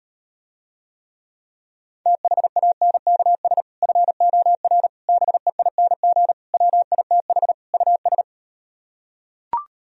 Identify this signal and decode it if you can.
Morse code, heard as a keyed tone, about 38 wpm, 700 Hz. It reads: THANKS FOR BEING WITH US